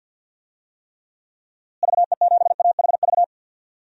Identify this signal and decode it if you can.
Morse code, heard as a keyed tone, about 52 wpm, 700 Hz. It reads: VE7AHV
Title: DXCC-100 Call Signs - 6 Characters - Repeated 2x - 50wpm